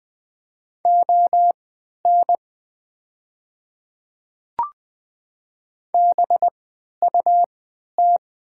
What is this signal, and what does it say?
Morse code, heard as a keyed tone, about 20 wpm, spaced as for 11 wpm, 700 Hz. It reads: ON BUT
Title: 3x Character Spacing - Top 100 Words - 20wpm